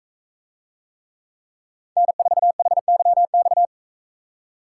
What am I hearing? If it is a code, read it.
Morse code, heard as a keyed tone, about 42 wpm, 700 Hz. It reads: N4HYX